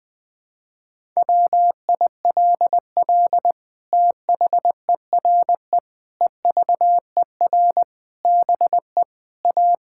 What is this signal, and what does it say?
Morse code, heard as a keyed tone, about 20 wpm, 700 Hz. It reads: WILL THERE EVER BE A
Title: Sentences from Top 1000 Words - Review Part 5 - 20wpm